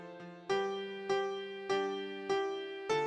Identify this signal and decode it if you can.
There is no code.